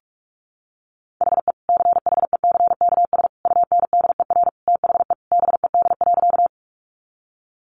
Morse code, heard as a keyed tone, about 44 wpm, 700 Hz. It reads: HE CHECKS UNDER THE BED.
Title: Easy Phrases 45wpm